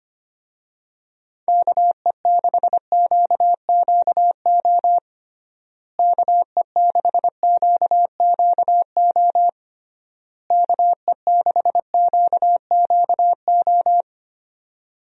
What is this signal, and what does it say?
Morse code, heard as a keyed tone, about 25 wpm, 700 Hz. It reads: KE6QQO KE6QQO KE6QQO
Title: USA Call Signs - 6 Characters - Repeated 3x - 25wpm